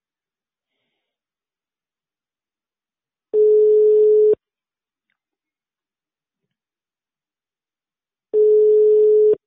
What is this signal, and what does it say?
Ringback tone (Europe, 425 Hz).